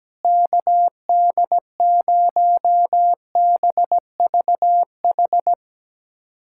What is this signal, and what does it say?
Morse code, heard as a keyed tone, about 17 wpm, 700 Hz. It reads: KD0BVH
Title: USA Callsign DE Callsign - 6 Characters - 17wpm